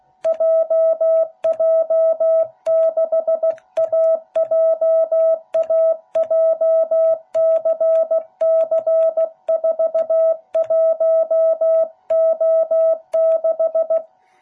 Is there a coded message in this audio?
Yes: Morse code from 0.3 to 14.0 s, heard as a keyed tone, about 16 wpm, 660 Hz. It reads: JJ6AJA JCC41O6